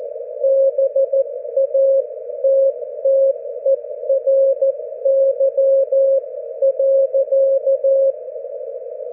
Morse code, heard as a keyed tone, about 14 wpm, spaced as for 12 wpm, 540 Hz. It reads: BATTERY.
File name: GW3UEP / GW4HXO 40mWRF 42mls 24Jly2012